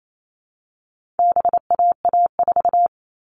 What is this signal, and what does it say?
Morse code, heard as a keyed tone, about 28 wpm, 700 Hz. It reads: BAA4
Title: All Cut Numbers - 4-Digit 28wpm